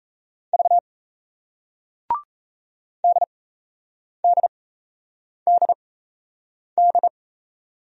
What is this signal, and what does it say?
Morse code, heard as a keyed tone, about 38 wpm, spaced as for 8 wpm, 700 Hz. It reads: V DDDD